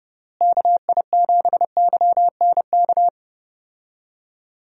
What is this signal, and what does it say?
Morse code, heard as a keyed tone, about 30 wpm, 700 Hz. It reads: KI7YNK
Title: USA Call Signs - 6 Characters - Repeated 3x - 30wpm